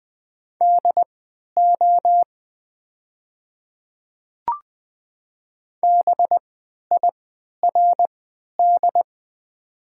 Morse code, heard as a keyed tone, about 20 wpm, spaced as for 11 wpm, 700 Hz. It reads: DO BIRD
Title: 3x Character Spacing - Top 500 Words - 20wpm